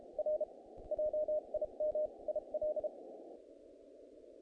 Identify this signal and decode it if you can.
Morse code, heard as a keyed tone, about 32 wpm, spaced as for 28 wpm, 610 Hz. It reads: R JIM IL
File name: contests / naqp-cw-k5tr-op-k5pi- -1800